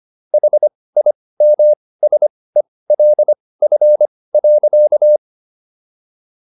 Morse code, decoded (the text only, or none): HIMSELF.